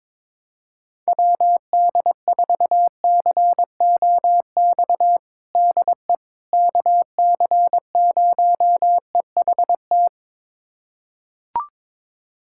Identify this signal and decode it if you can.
Morse code, heard as a keyed tone, about 22 wpm, 700 Hz. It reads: WD4COX DE KC0EHT
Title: USA Callsign DE Callsign 22wpm